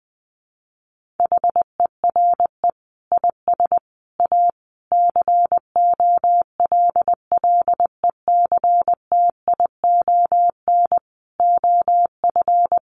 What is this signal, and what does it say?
Morse code, heard as a keyed tone, about 20 wpm, 700 Hz. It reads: HERE IS A COLLECTION OF
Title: Sentences from Top 1200 Words 20wpm